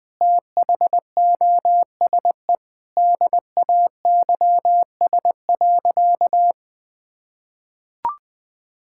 Morse code, heard as a keyed tone, about 20 wpm, 700 Hz. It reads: THOSE DAYS.
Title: Sentences with 4-Word Phrases - 20wpm